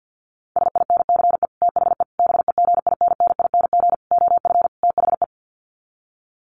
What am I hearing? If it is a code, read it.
Morse code, heard as a keyed tone, about 50 wpm, 700 Hz. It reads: SINCE THE BEGINNING OF THE
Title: Sets of 5 Words Encore2 50wpm